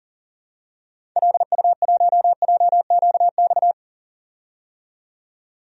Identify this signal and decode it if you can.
Morse code, heard as a keyed tone, about 38 wpm, 700 Hz. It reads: LU1JQX
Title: DXCC-100 Call Signs - 6 Characters - 40wpm